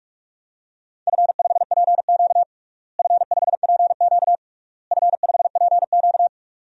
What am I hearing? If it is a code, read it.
Morse code, heard as a keyed tone, about 46 wpm, 700 Hz. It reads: F5PQ F5PQ F5PQ
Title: DXCC-100 Call Signs - 4 Characters - Repeated 3x - 45wpm